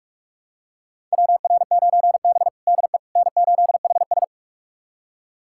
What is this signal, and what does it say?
Morse code, heard as a keyed tone, about 42 wpm, 690 Hz. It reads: WR9B DE N8HS
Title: USA Callsign DE Callsign - 4 Characters - 45wpm